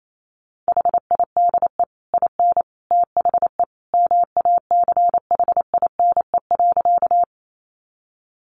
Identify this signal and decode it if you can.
Morse code, heard as a keyed tone, about 28 wpm, 700 Hz. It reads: HIDE IN THE MACHINE.
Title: A Sound Of Thunder 28wpm